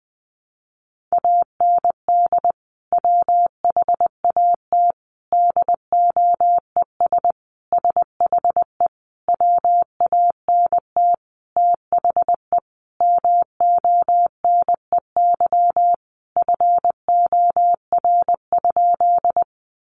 Morse code, heard as a keyed tone, about 19 wpm, 700 Hz. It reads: AND WHAT DOES SHE WANT THE MONEY FOR?
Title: Sentences from Top 400 Words 20wpm